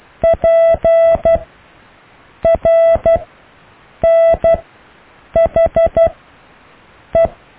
Morse code, heard as a keyed tone, about 12 wpm, spaced as for 6 wpm, 660 Hz. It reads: PRNHE
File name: cw / EKTPV+BFHNW